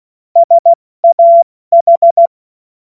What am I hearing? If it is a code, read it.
Morse code, heard as a keyed tone, about 16 wpm, 680 Hz. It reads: SAH
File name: Morsekurs / Witze / Witz 07. März 2023 70 F80